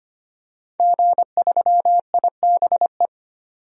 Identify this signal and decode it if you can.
Morse code, heard as a keyed tone, about 25 wpm, 700 Hz. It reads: G3IBE
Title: DXCC-100 Call Signs - 5 Characters - 25wpm